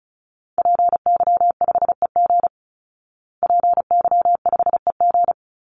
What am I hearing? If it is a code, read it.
Morse code, heard as a keyed tone, about 34 wpm, 700 Hz. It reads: PY5EG PY5EG
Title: Call Signs Repeated 2x - 35wpm